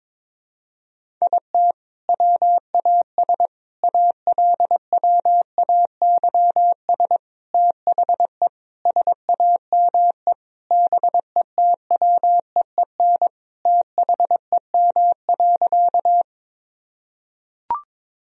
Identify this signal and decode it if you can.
Morse code, heard as a keyed tone, about 22 wpm, 700 Hz. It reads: IT WAS ALWAYS THE SAME BETWEEN THEM.
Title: Sentences from Top 1000 Words - Review Part 1 - 22wpm